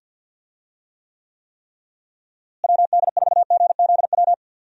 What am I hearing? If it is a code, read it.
Morse code, heard as a keyed tone, about 51 wpm, 700 Hz. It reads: WD4GZW
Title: USA Call Signs - 6 Characters - Repeated 3x - 50wpm